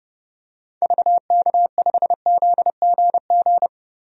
Morse code, heard as a keyed tone, about 30 wpm, 700 Hz. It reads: VK5ZGG
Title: DXCC-100 Call Signs - 6 Characters - 30wpm